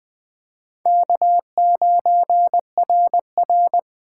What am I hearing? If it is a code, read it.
Morse code, heard as a keyed tone, about 21 wpm, 700 Hz. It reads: K9RR